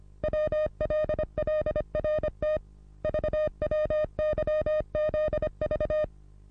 Morse code, heard as a keyed tone, about 26 wpm, 610 Hz. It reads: WLLRT VWYZV
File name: repos / Morsekurs / DJ4UF Morsen27